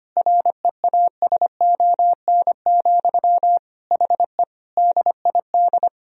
Morse code, heard as a keyed tone, about 25 wpm, 700 Hz. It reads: REASON, HE DID
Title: Sentences from Top 1000 Words - Review Part 3 - 25wpm